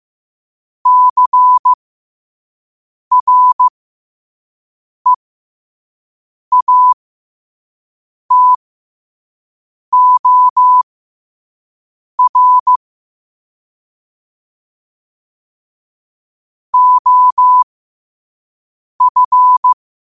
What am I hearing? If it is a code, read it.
Morse code, heard as a keyed tone, about 15 wpm, spaced as for 5 wpm, 1000 Hz. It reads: CREATOR OF
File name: qotdfiles / 5wpm